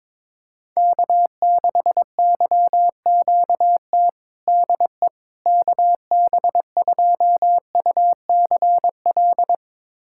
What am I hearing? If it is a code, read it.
Morse code, heard as a keyed tone, about 22 wpm, 700 Hz. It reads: K6YQT DE KB2UCL
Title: USA Callsign DE Callsign 22wpm